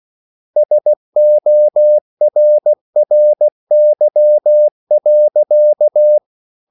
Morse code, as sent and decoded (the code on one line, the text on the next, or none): ... --- .-. .-. -.-- .-.-.-
SORRY.